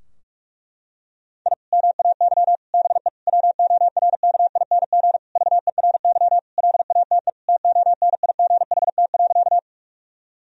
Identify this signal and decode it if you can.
Morse code, heard as a keyed tone, about 46 wpm, 700 Hz. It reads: I MAY BE WORKING VERY LATE TONIGHT.
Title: Sentences from Top 1000 Words - Review Part 4 - 45wpm